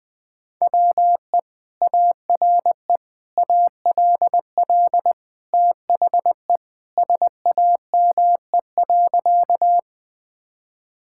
Morse code, heard as a keyed tone, about 20 wpm, 700 Hz. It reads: WE ARE ALL THE SAME.
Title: Sentences with 2-Word Phrases - 20wpm